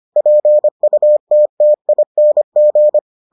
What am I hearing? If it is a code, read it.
Morse code, heard as a keyed tone, about 25 wpm, 600 Hz. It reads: PUTTING